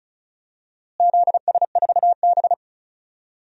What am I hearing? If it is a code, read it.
Morse code, heard as a keyed tone, about 36 wpm, 700 Hz. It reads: ZS4B
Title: DXCC-100 Call Signs - 4 Characters - Repeated 2x - 35wpm